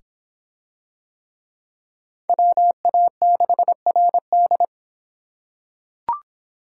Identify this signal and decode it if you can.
Morse code, heard as a keyed tone, about 26 wpm, 700 Hz. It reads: WA6RD